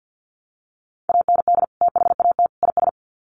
Morse code, heard as a keyed tone, about 50 wpm, 700 Hz. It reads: AND THAT IS
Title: Sets of 3 Words 50wpm